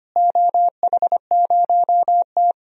Morse code, heard as a keyed tone, about 25 wpm, 700 Hz. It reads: OH0T
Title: DXCC-100 Call Signs - 4 Characters - Repeated 2x - 25wpm